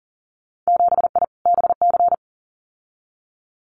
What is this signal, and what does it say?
Morse code, heard as a keyed tone, about 39 wpm, 700 Hz. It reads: 7I BC
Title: Winter Field Day Exchanges Encore 40wpm